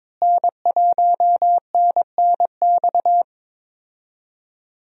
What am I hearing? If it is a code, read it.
Morse code, heard as a keyed tone, about 22 wpm, 700 Hz. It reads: N1NNX
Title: USA Call Signs - 5 Characters - Repeated 3x - 22wpm